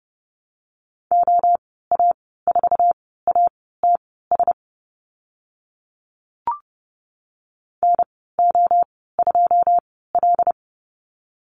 Morse code, heard as a keyed tone, about 30 wpm, spaced as for 17 wpm, 700 Hz. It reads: OA4ATS NO2L
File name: Lesson 14g - Number 2 - Call Signs 30wpm